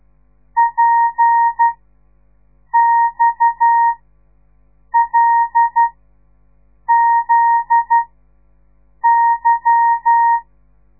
Morse code, heard as a keyed tone, about 12 wpm, spaced as for 6 wpm, 930 Hz. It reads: PXLZY
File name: radiomorningstar / morse / les 8